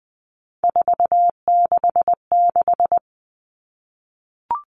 Morse code, heard as a keyed tone, about 20 wpm, 700 Hz. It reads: 466
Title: Numbers - 3-Digit 20wpm